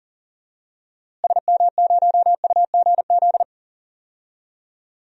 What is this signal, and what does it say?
Morse code, heard as a keyed tone, about 40 wpm, 700 Hz. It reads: SM0UGZ
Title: DXCC-100 Call Signs - 6 Characters - Repeated 3x - 40wpm